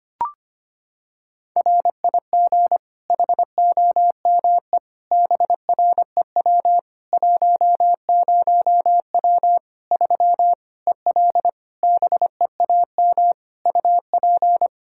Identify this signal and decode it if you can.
Morse code, heard as a keyed tone, about 25 wpm, 700 Hz. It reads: RIG HOME BREW 10W 3 EL BEAM UP
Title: K1USN SST QSO Practice 25wpm